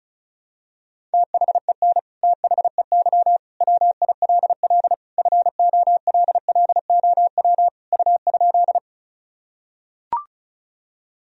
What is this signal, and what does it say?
Morse code, heard as a keyed tone, about 34 wpm, 700 Hz. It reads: THEN THEY WILL FOLLOW U?
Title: Sentences from Top 200 Words and Abbreviations 35wpm